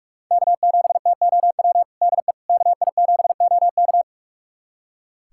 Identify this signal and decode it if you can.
Morse code, heard as a keyed tone, about 41 wpm, 700 Hz. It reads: K7TOW DE KI7OK
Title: USA Callsign DE Callsign - 5 Characters - 45wpm